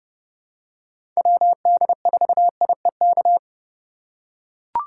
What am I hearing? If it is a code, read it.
Morse code, heard as a keyed tone, about 30 wpm, 700 Hz. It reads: WD4IEK
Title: USA Call Signs - 6 Characters - 4x Character Spacing - 30wpm